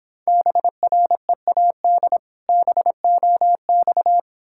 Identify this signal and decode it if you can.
Morse code, heard as a keyed tone, about 26 wpm, 700 Hz. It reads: BREAD BOX